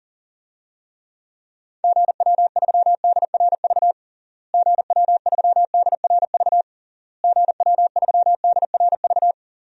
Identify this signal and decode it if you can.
Morse code, heard as a keyed tone, about 40 wpm, 700 Hz. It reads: GW3DRV GW3DRV GW3DRV
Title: DXCC-100 Call Signs - 6 Characters - Repeated 3x - 40wpm